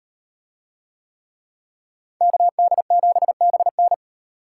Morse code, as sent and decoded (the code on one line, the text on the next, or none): -.- -.. --... -... -.
KD7BN